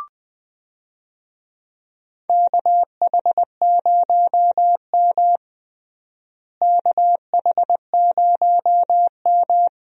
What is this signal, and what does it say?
Morse code, heard as a keyed tone, about 21 wpm, 700 Hz. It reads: KH0M KH0M
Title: USA Call Signs - 4 Characters - Repeated 2x - 20wpm